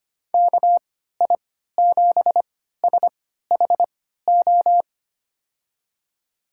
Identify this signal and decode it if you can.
Morse code, heard as a keyed tone, about 25 wpm, spaced as for 14 wpm, 700 Hz. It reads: KI7SHO
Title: USA Call Signs - 6 Characters - 3x Character Spacing - 25wpm